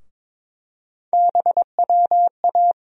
Morse code, heard as a keyed tone, about 21 wpm, 700 Hz. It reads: BWA
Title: USA Call Sign Postfixes - 3 Letters 22wpm